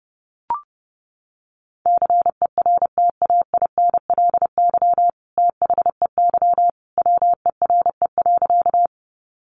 Morse code, heard as a keyed tone, about 30 wpm, 700 Hz. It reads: CERTAINLY THEY WERE.
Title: Sentences from Top 900 Words 30wpm